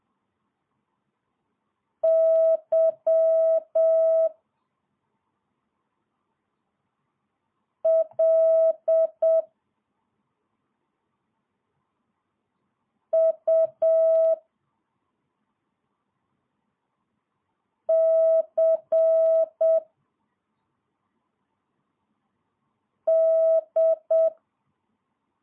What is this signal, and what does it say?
Morse code, heard as a keyed tone, about 7 wpm, 650 Hz. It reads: Y L U C D
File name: jh8cbh / 2022cw